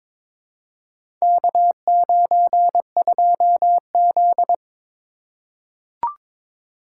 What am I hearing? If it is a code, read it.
Morse code, heard as a keyed tone, about 22 wpm, 700 Hz. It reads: K92Z
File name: practice / 22wpm / Four Letter-Number Combinations - 22wpm